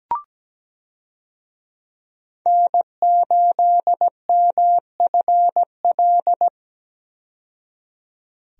Morse code, heard as a keyed tone, about 17 wpm, 700 Hz. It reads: N8MFL
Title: USA Call Signs - 5 Characters - 17wpm